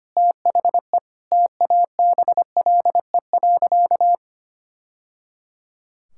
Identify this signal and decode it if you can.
Morse code, heard as a keyed tone, about 25 wpm, 700 Hz. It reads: THE TABLE.
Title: Sentences from Top 1000 Words - Review Part 6 - 25wpm